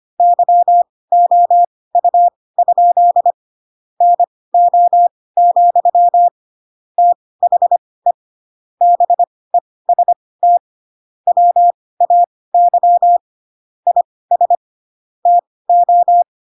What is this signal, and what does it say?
Morse code, heard as a keyed tone, about 25 wpm, spaced as for 18 wpm, 700 Hz. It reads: YOU? NO, THE BEST WAY IS TO
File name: Pinocchio / Pinocchio ch5 12 25 ik1hge